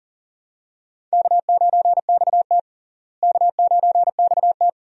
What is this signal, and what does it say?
Morse code, heard as a keyed tone, about 40 wpm, 700 Hz. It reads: K9XT K9XT
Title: USA Call Signs - 4 Characters - Repeated 2x - 40wpm